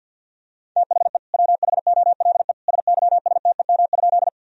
Morse code, heard as a keyed tone, about 51 wpm, 700 Hz. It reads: THE WHOLE SYSTEM?
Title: Sentences from Top 1000 Words - Review Part 3 - 50wpm